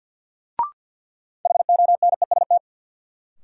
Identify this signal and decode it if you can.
Morse code, heard as a keyed tone, about 51 wpm, 700 Hz. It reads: HONEST
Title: Sets of 5 Words Encore - New Word Drill - 50wpm